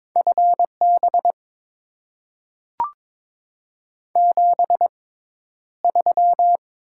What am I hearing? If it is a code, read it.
Morse code, heard as a keyed tone, about 22 wpm, spaced as for 11 wpm, 700 Hz. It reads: FB 73
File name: Top 20 QSO Elements - 6x Character Spacing - 22wpm